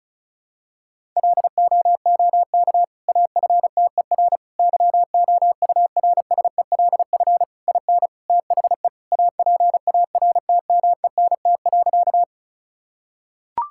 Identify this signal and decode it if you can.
Morse code, heard as a keyed tone, about 36 wpm, 700 Hz. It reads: LOOK AFTER YOURSELF IN THE APARTMENT.
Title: Sentences from Top 1000 Words - Review Part 6 - 35wpm